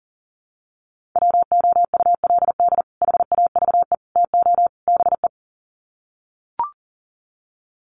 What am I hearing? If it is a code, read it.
Morse code, heard as a keyed tone, about 40 wpm, 700 Hz. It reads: WOULD HAVE TO BE
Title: Sets of 4 Words 40wpm